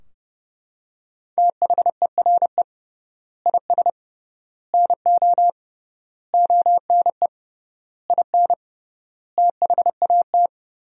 Morse code, heard as a keyed tone, about 30 wpm, 700 Hz. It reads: THERE IS NO ONE IN THAT
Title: 3x Word-Spacing - Sentences from Top 500 Words - 30wpm